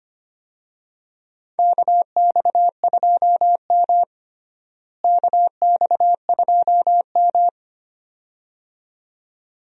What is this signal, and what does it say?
Morse code, heard as a keyed tone, about 25 wpm, 700 Hz. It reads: KX2M KX2M
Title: USA Call Signs - 4 Characters - Repeated 2x - 25wpm